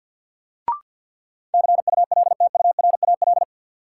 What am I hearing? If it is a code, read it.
Morse code, heard as a keyed tone, about 53 wpm, 700 Hz. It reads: CULTURAL
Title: Sentences from Top 1500 Words - New Word Drill Alt - 50wpm